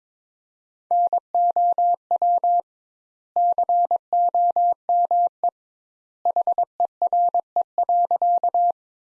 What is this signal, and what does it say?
Morse code, heard as a keyed tone, about 22 wpm, 700 Hz. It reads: NOW COME HERE.
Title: Sentences Top 200 Words - 22wpm